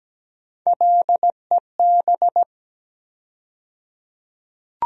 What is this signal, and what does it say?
Morse code, heard as a keyed tone, about 17 wpm, 700 Hz. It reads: LEB
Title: USA Call Sign Postfixes - 3 Letters 17wpm